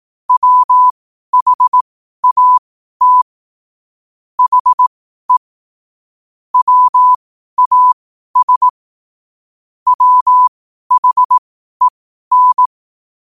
Morse code, heard as a keyed tone, about 18 wpm, spaced as for 12 wpm, 1000 Hz. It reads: WHAT HE WAS WHEN
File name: qotdfiles / 12wpm